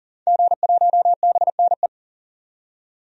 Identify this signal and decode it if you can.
Morse code, heard as a keyed tone, about 39 wpm, 700 Hz. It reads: G1BNE
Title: DXCC-100 Call Signs - 5 Characters - 40wpm